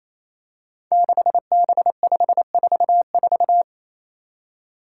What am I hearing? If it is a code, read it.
Morse code, heard as a keyed tone, about 27 wpm, 700 Hz. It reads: 6B544
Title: Five Challenging Letter-Number Combinations 28wpm